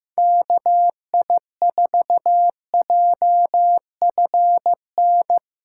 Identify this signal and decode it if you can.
Morse code, heard as a keyed tone, about 15 wpm, 700 Hz. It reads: KI4JFN
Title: USA Callsign DE Callsign - 6 Characters - 15wpm